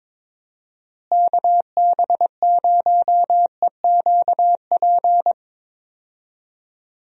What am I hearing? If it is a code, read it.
Morse code, heard as a keyed tone, about 22 wpm, 700 Hz. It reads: KB0EQP
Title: USA Call Signs - 6 Characters - Repeated 2x - 22wpm